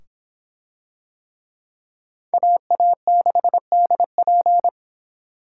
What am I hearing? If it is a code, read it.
Morse code, heard as a keyed tone, about 26 wpm, 700 Hz. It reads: AA6DP